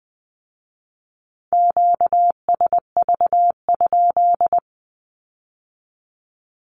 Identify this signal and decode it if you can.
Morse code, heard as a keyed tone, about 20 wpm, 700 Hz. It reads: QSV?
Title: Q-Codes with Spoken Meaning - 20wpm